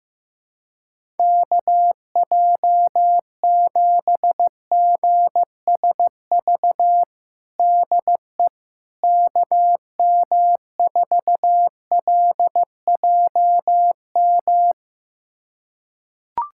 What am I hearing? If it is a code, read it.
Morse code, heard as a keyed tone, about 15 wpm, 700 Hz. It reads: KJ7GSV DE KM4LJM
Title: USA Callsign DE Callsign - 6 Characters - 15wpm